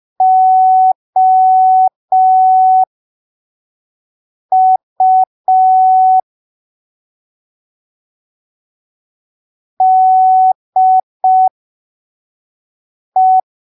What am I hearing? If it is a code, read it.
Morse code, heard as a keyed tone, about 5 wpm, spaced as for 3 wpm, 750 Hz. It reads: OU DE
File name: arquivos / texto / texto 09 05ppm